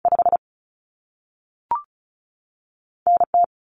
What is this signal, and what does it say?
Morse code, heard as a keyed tone, about 36 wpm, spaced as for 3 wpm, 700 Hz. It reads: 5N T
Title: Common Cut Numbers - 2-Digit 35wpm